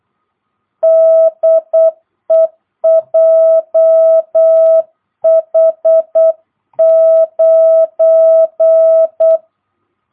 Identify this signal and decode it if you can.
Morse code, heard as a keyed tone, about 8 wpm, 660 Hz. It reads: DE JH9
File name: jh8cbh / 2022cw